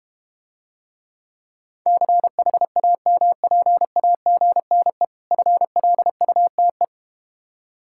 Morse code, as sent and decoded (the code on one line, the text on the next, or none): -.-. .... .- -- .--. .- --. -. / ..-. .-.. ..- - .
CHAMPAGNE FLUTE